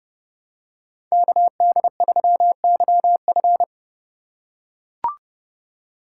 Morse code, heard as a keyed tone, about 31 wpm, 700 Hz. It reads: KD3YF